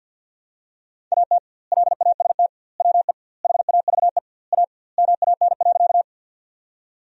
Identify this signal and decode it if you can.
Morse code, heard as a keyed tone, about 47 wpm, 700 Hz. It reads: AT LAST WE HAVE A MAN.
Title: 2x Word-Spacing - Sentences from Top 300 Words - 50wpm